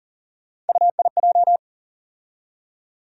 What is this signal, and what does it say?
Morse code, heard as a keyed tone, about 40 wpm, 700 Hz. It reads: UIJ
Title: USA Call Sign Postfixes - 3 Letters 40wpm